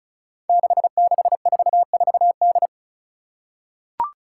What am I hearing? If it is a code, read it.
Morse code, heard as a keyed tone, about 34 wpm, 700 Hz. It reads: 6644D
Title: Five Challenging Letter-Number Combinations 35wpm